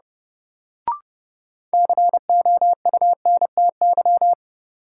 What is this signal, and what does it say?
Morse code, heard as a keyed tone, about 30 wpm, 700 Hz. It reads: COUNTY
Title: Sentences from Top 1200 Words - New Word Drill - 30wpm